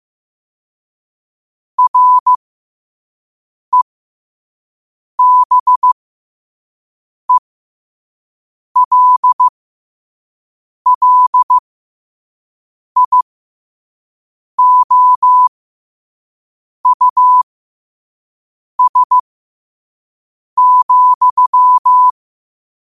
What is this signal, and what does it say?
Morse code, heard as a keyed tone, about 15 wpm, spaced as for 5 wpm, 1000 Hz. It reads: REBELLIOUS,
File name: qotdfiles / 5wpm